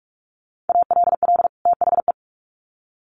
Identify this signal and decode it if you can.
Morse code, heard as a keyed tone, about 44 wpm, 700 Hz. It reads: ALL THE
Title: Sets of 2 Words 45wpm